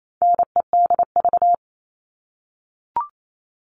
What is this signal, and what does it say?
Morse code, heard as a keyed tone, about 28 wpm, 700 Hz. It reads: NEDV
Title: All Cut Numbers - 4-Digit 28wpm